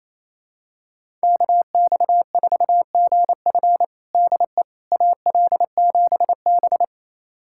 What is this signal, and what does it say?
Morse code, heard as a keyed tone, about 28 wpm, 700 Hz. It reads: KX4GF DE AL7B
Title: USA Callsign DE Callsign 28wpm